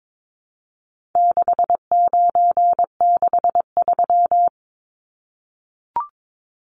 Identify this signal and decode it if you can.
Morse code, heard as a keyed tone, about 22 wpm, 700 Hz. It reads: 6963
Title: Numbers - 4-Digit 22wpm